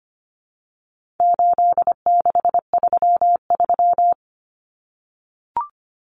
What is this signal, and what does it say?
Morse code, heard as a keyed tone, about 25 wpm, 700 Hz. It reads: 8633
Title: Numbers - 4-Digit 25wpm